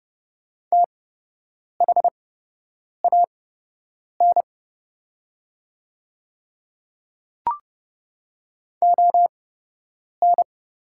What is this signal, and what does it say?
Morse code, heard as a keyed tone, about 30 wpm, spaced as for 8 wpm, 700 Hz. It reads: THAN ON
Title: 8x Character Spacing - Top 200 Words - 30wpm